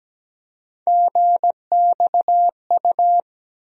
Morse code, heard as a keyed tone, about 17 wpm, 700 Hz. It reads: GXU
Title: USA Call Sign Postfixes - 3 Letters 17wpm